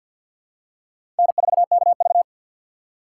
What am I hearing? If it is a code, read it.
Morse code, heard as a keyed tone, about 48 wpm, 700 Hz. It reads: N4KV
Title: USA Call Signs - 4 Characters - 3x Character Spacing - 50wpm